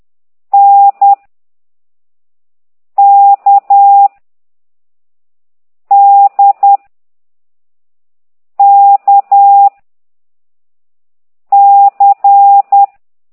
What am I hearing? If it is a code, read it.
Morse code, heard as a keyed tone, about 10 wpm, 800 Hz. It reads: N K D K C
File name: ct2idw / exerc4